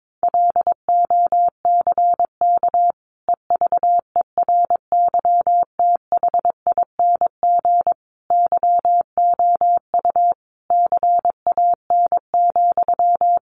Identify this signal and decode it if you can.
Morse code, heard as a keyed tone, about 22 wpm, 700 Hz. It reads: LOCK EVERYTHING YOU CAN,
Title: Sentences from Top 1300 Words 22wpm